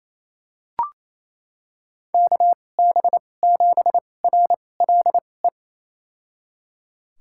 Morse code, heard as a keyed tone, about 28 wpm, spaced as for 20 wpm, 700 Hz. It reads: KB7RLE
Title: USA Call Signs - 6 Characters - 2x Character Spacing - 28wpm